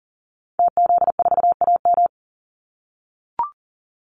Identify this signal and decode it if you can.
Morse code, heard as a keyed tone, about 40 wpm, 700 Hz. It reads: TZ4AM
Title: Call Signs - 40wpm